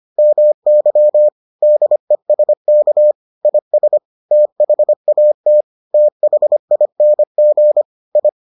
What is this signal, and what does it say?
Morse code, heard as a keyed tone, about 25 wpm, 600 Hz. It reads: MY DESK IS THAT THING I